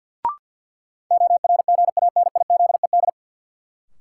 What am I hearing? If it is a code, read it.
Morse code, heard as a keyed tone, about 53 wpm, 700 Hz. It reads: ORGANIZED
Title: Sentences from Top 1300 Words - New Word Drill Alt - 50wpm